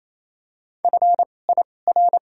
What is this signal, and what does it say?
Morse code, heard as a keyed tone, about 28 wpm, spaced as for 20 wpm, 700 Hz. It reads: FIR
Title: 2x Character Spacing - Top 500 Words - 28wpm